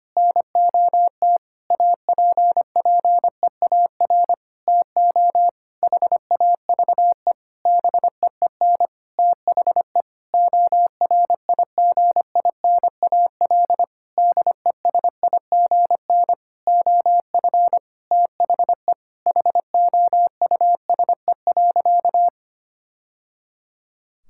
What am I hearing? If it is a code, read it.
Morse code, heard as a keyed tone, about 25 wpm, 700 Hz. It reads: NOT APPEAR TO HAVE BEEN THE ORIGINAL DESIGN OF THE HOUSE.
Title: Sentences from Top 1000 Words - Review Part 6 - 25wpm